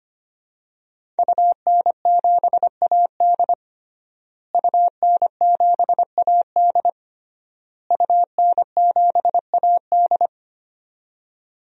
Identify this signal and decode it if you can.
Morse code, heard as a keyed tone, about 25 wpm, 700 Hz. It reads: UN7AD UN7AD UN7AD
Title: DXCC-100 Call Signs - 5 Characters - Repeated 3x - 25wpm